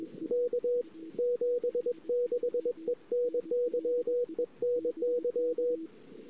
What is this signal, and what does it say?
Morse code, heard as a keyed tone, about 21 wpm, 490 Hz. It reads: K 76 ENY ENY
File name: CW / 2008-ss-cw-k5tr-op-n5rz -22:00